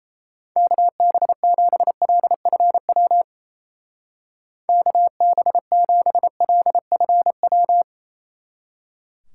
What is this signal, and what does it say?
Morse code, heard as a keyed tone, about 34 wpm, 700 Hz. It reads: KB7LFW KB7LFW
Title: Speed Racing - USA 2x3 Call Signs 28wpm